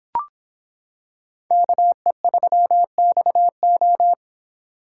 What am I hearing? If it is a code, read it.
Morse code, heard as a keyed tone, about 25 wpm, 700 Hz. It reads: KE3XO